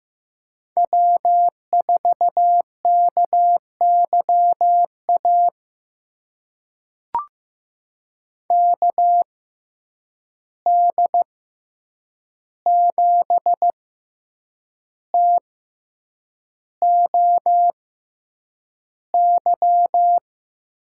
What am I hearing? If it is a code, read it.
Morse code, heard as a keyed tone, about 15 wpm, spaced as for 7 wpm, 700 Hz. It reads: W4KYA KD7TOY
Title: USA Call Signs - 6x Character Spacing - 15wpm